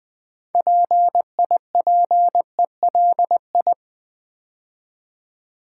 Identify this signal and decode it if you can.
Morse code, heard as a keyed tone, about 20 wpm, 700 Hz. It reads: PIPELI
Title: Build A Word - 3 Letters At A Time - 8-Letter Words - 20wpm